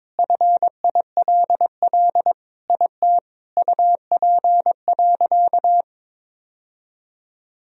Morse code, heard as a keyed tone, about 22 wpm, 700 Hz. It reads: FILL IT UP.
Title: Sentences from Top 1000 Words - Review Part 3 - 22wpm